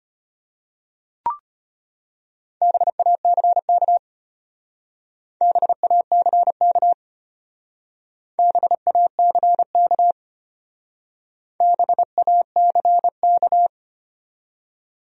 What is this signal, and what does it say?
Morse code, heard as a keyed tone, about 32 wpm, 700 Hz. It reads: BACK BACK BACK BACK